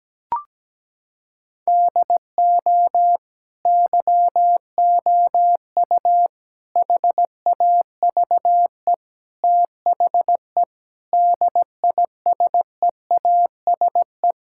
Morse code, heard as a keyed tone, about 17 wpm, 700 Hz. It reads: DO YOU HAVE THE DISEASE
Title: Sentences from Top 1000 Words - Review Part 6 - 17wpm